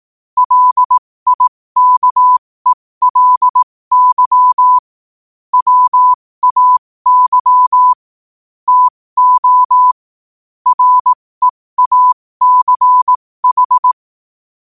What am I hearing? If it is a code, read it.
Morse code, heard as a keyed tone, about 18 wpm, spaced as for 15 wpm, 1000 Hz. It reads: LIKELY WAY TO REACH
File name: qotdfiles / 15wpm